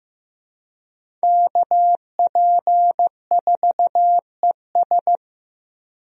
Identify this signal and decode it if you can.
Morse code, heard as a keyed tone, about 15 wpm, 700 Hz. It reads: KP4ES
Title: USA Call Signs - 5 Characters - Repeated 3x - 15wpm